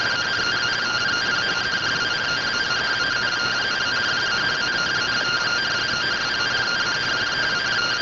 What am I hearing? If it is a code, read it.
Two-tone teleprinter signal, about 170 Hz shift.